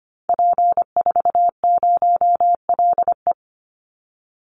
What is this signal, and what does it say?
Morse code, heard as a keyed tone, about 25 wpm, 700 Hz. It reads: P40LE
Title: Call Signs Repeated 2x - 25wpm